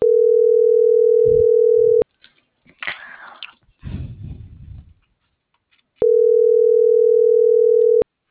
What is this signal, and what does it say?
Ringback tone (North America), 440 + 480 Hz.